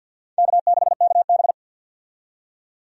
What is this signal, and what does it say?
Morse code, heard as a keyed tone, about 53 wpm, 700 Hz. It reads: K6KB
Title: USA Call Signs - 4 Characters - 50wpm